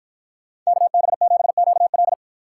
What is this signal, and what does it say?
Morse code, heard as a keyed tone, about 53 wpm, 700 Hz. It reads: KB7QL